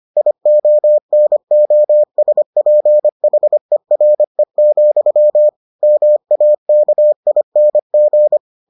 Morse code, heard as a keyed tone, about 25 wpm, 600 Hz. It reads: IONOSPHERE, MAKING